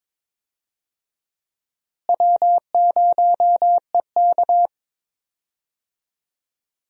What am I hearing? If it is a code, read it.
Morse code, heard as a keyed tone, about 21 wpm, 700 Hz. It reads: W0EK
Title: USA Call Signs - 4 Characters - Repeated 3x - 22wpm